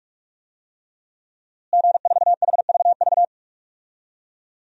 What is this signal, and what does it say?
Morse code, heard as a keyed tone, about 42 wpm, 700 Hz. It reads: G4HVV
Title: DXCC-100 Call Signs - 5 Characters - Repeated 3x - 45wpm